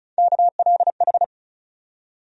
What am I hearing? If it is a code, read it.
Morse code, heard as a keyed tone, about 34 wpm, 700 Hz. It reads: KLH
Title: USA Call Sign Postfixes - 3 Letters 35wpm